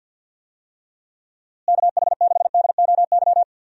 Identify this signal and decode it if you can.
Morse code, heard as a keyed tone, about 51 wpm, 700 Hz. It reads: KH6DOY
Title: DXCC-100 Call Signs - 6 Characters - Repeated 2x - 50wpm